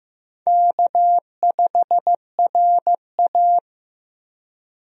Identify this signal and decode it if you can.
Morse code, heard as a keyed tone, about 15 wpm, 700 Hz. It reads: K5RA
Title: USA Call Signs - 4 Characters - Repeated 2x - 15wpm